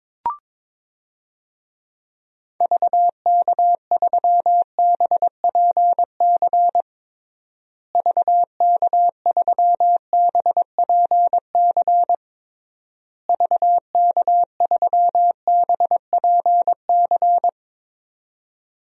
Morse code, heard as a keyed tone, about 21 wpm, 700 Hz. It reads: VK3BPC VK3BPC VK3BPC
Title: DXCC-100 Call Signs - 6 Characters - Repeated 3x - 22wpm